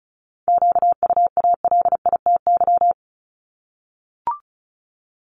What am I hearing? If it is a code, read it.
Morse code, heard as a keyed tone, about 35 wpm, 700 Hz. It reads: QUALITY
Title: Multi-letter Phonemes Repeated 3x 35wpm